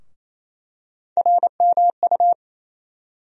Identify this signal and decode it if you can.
Morse code, heard as a keyed tone, about 28 wpm, 700 Hz. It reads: RMU
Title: USA Call Sign Postfixes - 3 Letters 28wpm